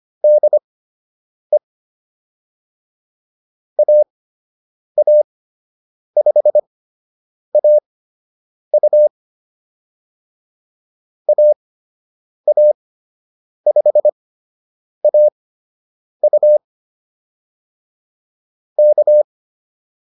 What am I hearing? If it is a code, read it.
Morse code, heard as a keyed tone, about 25 wpm, spaced as for 8 wpm, 600 Hz. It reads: DE AA5AU AA5AU K